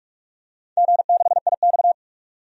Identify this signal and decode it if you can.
Morse code, heard as a keyed tone, about 46 wpm, 700 Hz. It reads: G6IX
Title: DXCC-100 Call Signs - 4 Characters - 45wpm